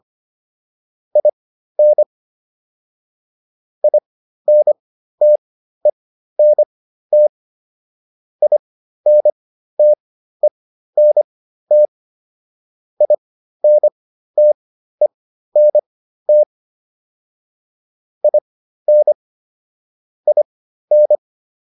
Morse code, heard as a keyed tone, about 25 wpm, spaced as for 13 wpm, 600 Hz. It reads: IN INTENT INTENT INTENT IN IN